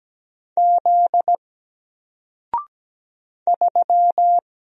Morse code, heard as a keyed tone, about 17 wpm, 700 Hz. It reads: Z 3
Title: Single Letter-Number 17wpm